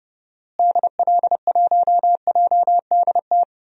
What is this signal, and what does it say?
Morse code, heard as a keyed tone, about 32 wpm, 700 Hz. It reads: DL1JDT
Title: DXCC-100 Call Signs - 6 Characters - Repeated 3x - 30wpm